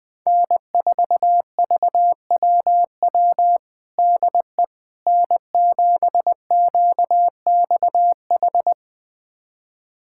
Morse code, heard as a keyed tone, about 20 wpm, 700 Hz. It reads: N4VWW DE N7QXH
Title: USA Callsign DE Callsign - 5 Characters - 20wpm